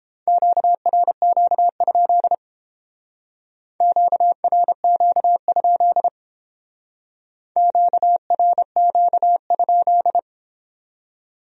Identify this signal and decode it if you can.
Morse code, heard as a keyed tone, about 30 wpm, 700 Hz. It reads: QRQ? QRQ? QRQ?